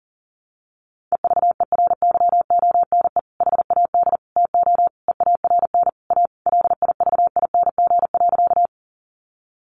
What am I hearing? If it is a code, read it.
Morse code, heard as a keyed tone, about 40 wpm, 700 Hz. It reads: EVERYONE HAD TO EARN A LIVING.
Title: Sentences from Top 1500 Words 40wpm